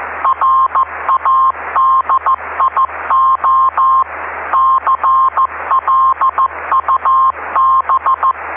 Morse code, heard as a keyed tone, about 15 wpm, 1060 Hz, with noise. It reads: RADIO CLUB
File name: ve3we-cw-beacon-28265